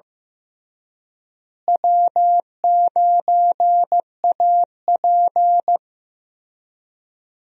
Morse code, heard as a keyed tone, about 15 wpm, 700 Hz. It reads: W9AP
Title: USA Call Signs - 4 Characters - Repeated 3x - 15wpm